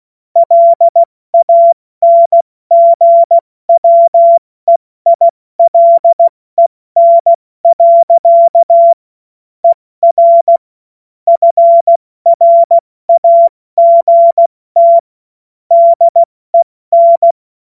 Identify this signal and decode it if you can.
Morse code, heard as a keyed tone, about 16 wpm, 680 Hz. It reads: LANGWEILEN. ER FRAGT DEN
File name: Witz-16-September-2025-70-F80